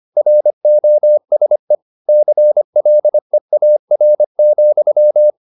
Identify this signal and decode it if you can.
Morse code, heard as a keyed tone, about 25 wpm, 600 Hz. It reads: ROSE CLEAR,
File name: the old man and the sea part 1 25wpm